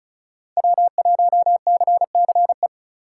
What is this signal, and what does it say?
Morse code, heard as a keyed tone, about 36 wpm, 700 Hz. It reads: W1CCE
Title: K1USN SST QSO Practice - 35wpm